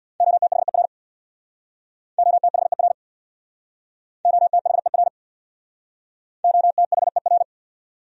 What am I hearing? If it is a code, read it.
Morse code, heard as a keyed tone, about 60 wpm, 700 Hz. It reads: OTHER OTHER OTHER OTHER
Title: Speed-Racing Top 300 Words 50wpm